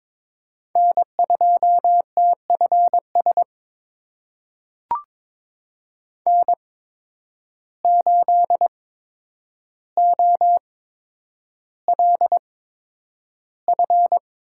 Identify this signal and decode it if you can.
Morse code, heard as a keyed tone, about 22 wpm, spaced as for 9 wpm, 700 Hz. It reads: N2TFS N8OLF
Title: USA Call Signs - 8x Character Spacing - 22wpm